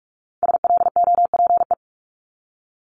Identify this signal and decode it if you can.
Morse code, heard as a keyed tone, about 46 wpm, 700 Hz. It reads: SLOPE
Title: Multi-letter Phonemes Repeated 3x 45wpm